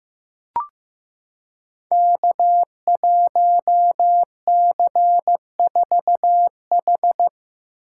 Morse code, heard as a keyed tone, about 15 wpm, 700 Hz. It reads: K1C4H
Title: Code Groups - 5 Letter:Numbers - 15wpm